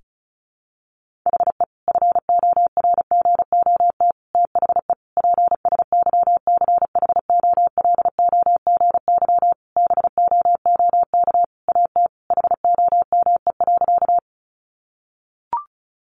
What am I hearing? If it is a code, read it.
Morse code, heard as a keyed tone, about 34 wpm, 700 Hz. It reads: HE FORGOT THE PSYCHOLOGY BOOK AT HOME.
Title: Sentences from Top 1400 Words 35wpm